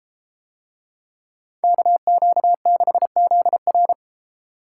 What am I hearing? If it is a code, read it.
Morse code, heard as a keyed tone, about 32 wpm, 700 Hz. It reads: KQ6ZR